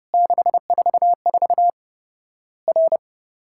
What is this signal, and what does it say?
Morse code, heard as a keyed tone, about 30 wpm, 700 Hz. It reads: 644 R
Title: Worked All Europe DX Contest - QTC Exchange - 30wpm